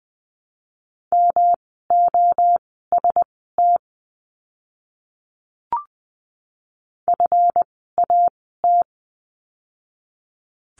Morse code, heard as a keyed tone, about 20 wpm, spaced as for 14 wpm, 700 Hz. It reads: MOST FAT
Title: 139 Lesson 23f - Letter f - Words 20wpm